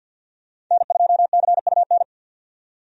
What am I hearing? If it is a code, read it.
Morse code, heard as a keyed tone, about 50 wpm, 700 Hz. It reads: N2CUN
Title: USA Call Signs - 5 Characters - 50wpm